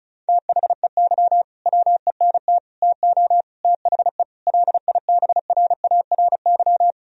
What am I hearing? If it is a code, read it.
Morse code, heard as a keyed tone, about 34 wpm, 700 Hz. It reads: THEY WENT TO THE LIBRARY
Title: Sentences from Top 1000 Words - Review Part 4 - 35wpm